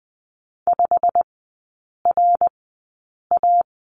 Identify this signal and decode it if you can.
Morse code, heard as a keyed tone, about 20 wpm, spaced as for 8 wpm, 690 Hz. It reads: 5RA
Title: 087 Lesson 15e - Number 5 - Three Characters 20wpm